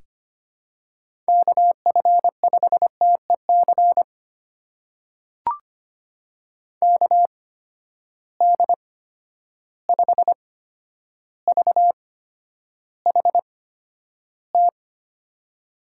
Morse code, heard as a keyed tone, about 25 wpm, spaced as for 11 wpm, 700 Hz. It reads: KF5TEC KD5VHT
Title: USA Call Signs - 6 Characters - 8x Character Spacing - 25wpm